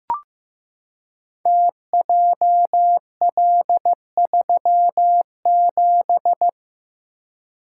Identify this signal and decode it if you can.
Morse code, heard as a keyed tone, about 15 wpm, 700 Hz. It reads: TJL37
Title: Code Groups - 5 Letter:Numbers - 15wpm